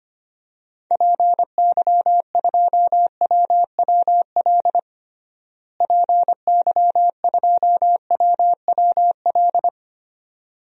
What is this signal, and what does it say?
Morse code, heard as a keyed tone, about 25 wpm, 700 Hz. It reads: PY2WWL PY2WWL
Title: DXCC-100 Call Signs - 6 Characters - Repeated 2x - 25wpm